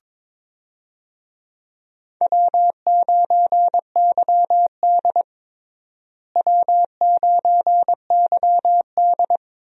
Morse code, heard as a keyed tone, about 22 wpm, 700 Hz. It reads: W9YD W9YD
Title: USA Call Signs - 4 Characters - Repeated 3x - 22wpm